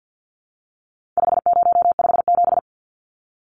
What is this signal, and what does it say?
Morse code, heard as a keyed tone, about 50 wpm, 700 Hz. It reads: 5057
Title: Numbers - 4-Digit 50wpm